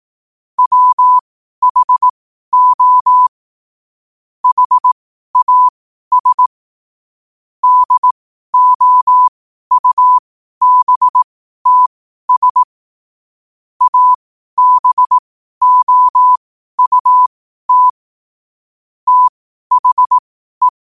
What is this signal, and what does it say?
Morse code, heard as a keyed tone, about 18 wpm, spaced as for 12 wpm, 1000 Hz. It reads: WHO HAS DOUBTS ABOUT THE
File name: qotdfiles / 12wpm